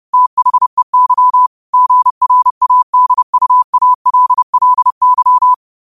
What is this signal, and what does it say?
Morse code, heard as a keyed tone, about 30 wpm, 1000 Hz. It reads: THEY GRADUALLY
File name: qotdfiles / 30wpm